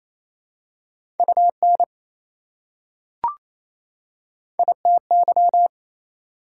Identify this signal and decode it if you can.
Morse code, heard as a keyed tone, about 28 wpm, 700 Hz. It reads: UN ITY
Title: Prefixes and Suffixes 28wpm